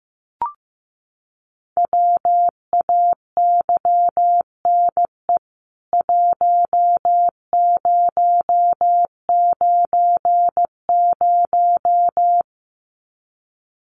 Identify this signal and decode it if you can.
Morse code, heard as a keyed tone, about 15 wpm, 700 Hz. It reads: WAYNE 1090
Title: CWT Exchange 15wpm